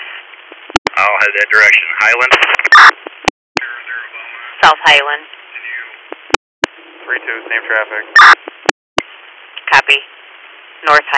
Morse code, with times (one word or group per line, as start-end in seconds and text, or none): none